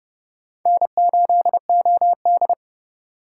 Morse code, heard as a keyed tone, about 38 wpm, 700 Hz. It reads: N8OD